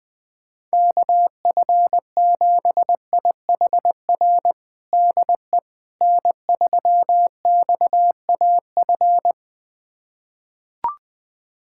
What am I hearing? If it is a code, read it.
Morse code, heard as a keyed tone, about 21 wpm, 700 Hz. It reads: KF7IHR DE N3XAF
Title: USA Callsign DE Callsign 20wpm